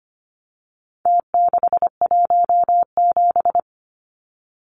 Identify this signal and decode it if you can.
Morse code, heard as a keyed tone, about 25 wpm, 700 Hz. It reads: T617
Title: Numbers - Contest Style - 4-Digit 25wpm